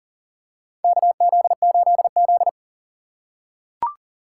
Morse code, heard as a keyed tone, about 40 wpm, 700 Hz. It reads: KZ8Z
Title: USA Call Signs - 4 Characters - 4x Character Spacing - 40wpm